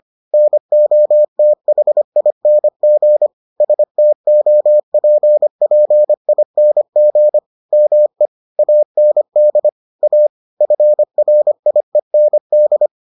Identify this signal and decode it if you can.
Morse code, heard as a keyed tone, about 25 wpm, 600 Hz. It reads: NOTHING STOPPING ME AND A FRIEND